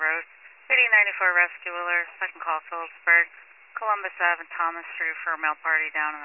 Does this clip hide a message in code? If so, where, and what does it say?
Yes: Emergency Alert System (SAME) tones from 1.0 to 5.1 s, heard as harsh buzzing bursts, with speech.